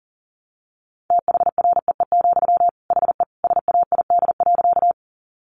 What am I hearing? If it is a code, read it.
Morse code, heard as a keyed tone, about 40 wpm, 700 Hz. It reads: THREE, HE SAID.
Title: Sentences from Top 200 Words 40wpm